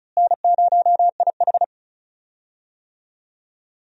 Morse code, heard as a keyed tone, about 36 wpm, 700 Hz. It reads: N0IH
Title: USA Call Signs - 4 Characters - Repeated 2x - 35wpm